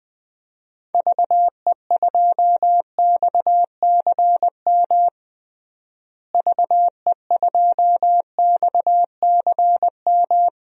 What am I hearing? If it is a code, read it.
Morse code, heard as a keyed tone, about 21 wpm, 700 Hz. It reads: VE2XCM VE2XCM
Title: DXCC-100 Call Signs - 6 Characters - Repeated 2x - 20wpm